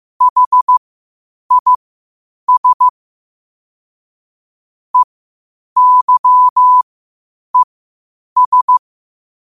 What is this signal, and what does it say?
Morse code, heard as a keyed tone, about 15 wpm, spaced as for 8 wpm, 1000 Hz. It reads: HIS EYES